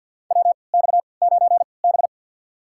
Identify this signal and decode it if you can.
Morse code, heard as a keyed tone, about 50 wpm, spaced as for 28 wpm, 700 Hz. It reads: WX9B
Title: USA Call Signs - 4 Characters - 3x Character Spacing - 50wpm